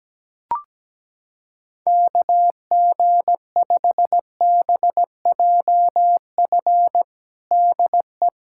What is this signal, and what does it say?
Morse code, heard as a keyed tone, about 17 wpm, 700 Hz. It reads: KG5BJF DE
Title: USA Callsign DE Callsign - 6 Characters - 17wpm